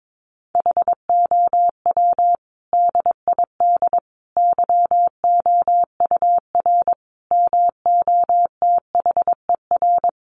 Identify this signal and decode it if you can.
Morse code, heard as a keyed tone, about 22 wpm, 700 Hz. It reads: HOW DID YOUR MOTHER
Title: Sentences from Top 700 Words 22wpm